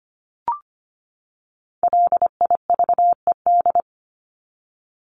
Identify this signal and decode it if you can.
Morse code, heard as a keyed tone, about 25 wpm, 700 Hz. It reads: LIVED
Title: ED Suffix - Top 100 Words with Base Words 25wpm